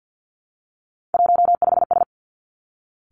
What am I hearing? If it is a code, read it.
Morse code, heard as a keyed tone, about 50 wpm, 700 Hz. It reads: 15S
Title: Three Challenging Letter-Number Combinations 50wpm